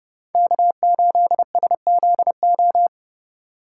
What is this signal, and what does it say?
Morse code, heard as a keyed tone, about 31 wpm, 700 Hz. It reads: K8SZO
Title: USA Call Signs - 5 Characters - 3x Character Spacing - 30wpm